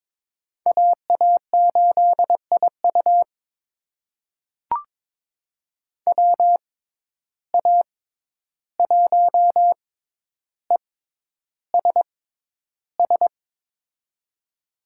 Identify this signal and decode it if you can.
Morse code, heard as a keyed tone, about 22 wpm, spaced as for 11 wpm, 700 Hz. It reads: AA8IU WA1ESS
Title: USA Call Signs - 6x Character Spacing - 22wpm